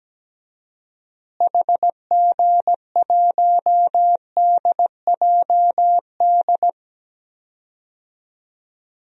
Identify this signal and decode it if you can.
Morse code, heard as a keyed tone, about 17 wpm, 700 Hz. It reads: HG1DJD
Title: DXCC-100 Call Signs - 6 Characters - Repeated 3x - 17wpm